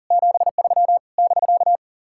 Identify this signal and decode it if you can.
Morse code, heard as a keyed tone, about 52 wpm, 700 Hz. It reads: 73 <BK>